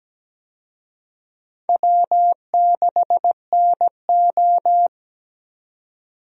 Morse code, heard as a keyed tone, about 17 wpm, 700 Hz. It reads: W6NO
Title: USA Call Signs - 4 Characters - 17wpm